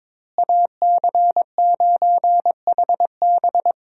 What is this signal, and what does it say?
Morse code, heard as a keyed tone, about 22 wpm, 700 Hz. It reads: AC9HB
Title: USA Call Signs - 5 Characters - Repeated 3x - 22wpm